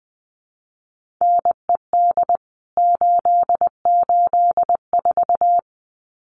Morse code, heard as a keyed tone, about 20 wpm, 700 Hz. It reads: NED 884
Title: CWT Exchange 20wpm